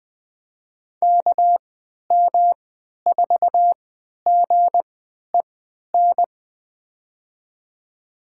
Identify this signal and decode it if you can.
Morse code, heard as a keyed tone, about 20 wpm, spaced as for 11 wpm, 700 Hz. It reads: KM4GEN
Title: USA Call Signs - 6 Characters - 3x Character Spacing - 20wpm